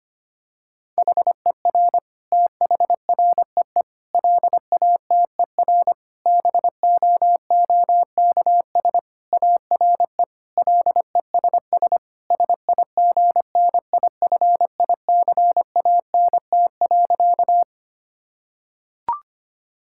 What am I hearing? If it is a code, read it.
Morse code, heard as a keyed tone, about 25 wpm, 700 Hz. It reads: HER THREE LATER BOOKS ARE LESS SIGNIFICANT.
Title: Sentences from Top 1000 Words - Review Part 6 - 25wpm